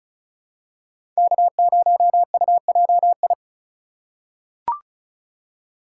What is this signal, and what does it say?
Morse code, heard as a keyed tone, about 36 wpm, 700 Hz. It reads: K0UJI
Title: USA Call Signs - 5 Characters - 4x Character Spacing - 35wpm